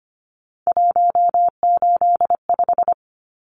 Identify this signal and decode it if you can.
Morse code, heard as a keyed tone, about 25 wpm, 690 Hz. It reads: 185
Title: Numbers - 3-Digit 25wpm